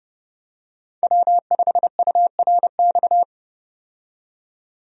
Morse code, heard as a keyed tone, about 30 wpm, 700 Hz. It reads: W5URX
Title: USA Call Signs - 5 Characters - 30wpm